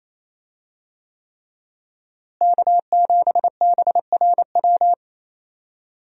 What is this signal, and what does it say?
Morse code, heard as a keyed tone, about 28 wpm, 700 Hz. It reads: K7BRW
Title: USA Call Signs - 5 Characters - Repeated 3x - 28wpm